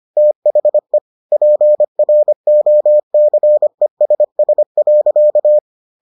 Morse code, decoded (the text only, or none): THE PROCESS.